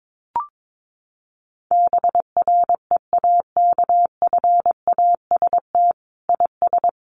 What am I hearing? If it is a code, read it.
Morse code, heard as a keyed tone, about 22 wpm, 700 Hz. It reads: BREAKFAST IS
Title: Sentences from Top 1500 Words 22wpm